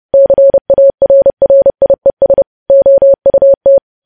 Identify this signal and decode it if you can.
Morse code, heard as a keyed tone, about 30 wpm, 550 Hz. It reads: CARRIES OUT
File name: podcast / News headlines-1x-30x30